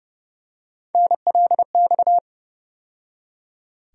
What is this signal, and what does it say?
Morse code, heard as a keyed tone, about 30 wpm, 700 Hz. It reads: NLX
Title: USA Call Sign Postfixes - 3 Letters 30wpm